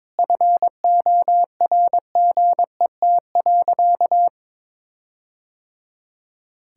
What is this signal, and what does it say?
Morse code, heard as a keyed tone, about 22 wpm, 700 Hz. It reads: FORGET.
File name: practice / 22wpm / Sentences with 4-Word Phrases - 22wpm